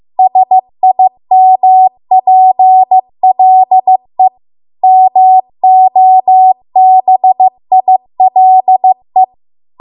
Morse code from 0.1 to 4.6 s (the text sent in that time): SIMPLE